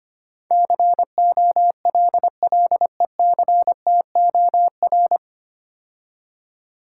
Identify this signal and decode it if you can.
Morse code, heard as a keyed tone, about 25 wpm, 700 Hz. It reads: COLLECTOR
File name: practice / 25wpm / 9 Letter Words - First 3 Letters Given - 25wpm